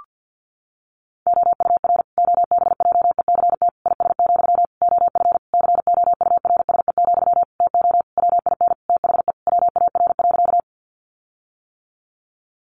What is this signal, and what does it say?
Morse code, heard as a keyed tone, about 50 wpm, 700 Hz. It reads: OUR OBJECT IS, OF COURSE, TO WIN THE WAR.
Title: Sentences from Top 1500 Words 50wpm